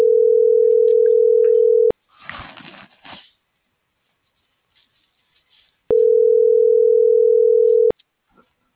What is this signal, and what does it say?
Ringback tone (North America), 440 + 480 Hz.